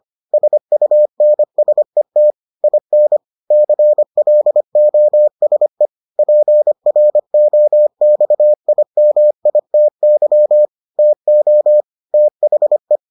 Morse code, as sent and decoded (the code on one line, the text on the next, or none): ... ..- -. ... - / .. -. / -.-. .-.. --- ... / .--. .-. --- -..- .. -- .. - -.-- / - --- / - .... .
SUNSET IN CLOSE PROXIMITY TO THE